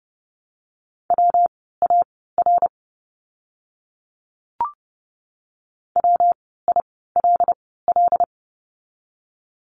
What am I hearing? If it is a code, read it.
Morse code, heard as a keyed tone, about 30 wpm, spaced as for 17 wpm, 700 Hz. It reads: WAR WILL
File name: course / 30:10wpm / 115 Lesson 19f - Letter w - Words 30wpm